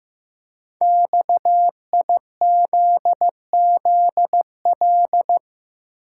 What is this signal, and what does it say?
Morse code, heard as a keyed tone, about 15 wpm, 700 Hz. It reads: XIZZL
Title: Code Groups - 5 Letters - 15wpm